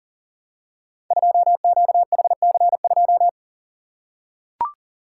Morse code, heard as a keyed tone, about 37 wpm, 700 Hz. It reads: 2QHC2
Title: Code Groups - 5 Letter:Numbers - 40wpm